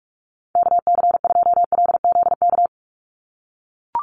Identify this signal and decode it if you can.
Morse code, heard as a keyed tone, about 44 wpm, 690 Hz. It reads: KC2LZK
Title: USA Call Signs 45wpm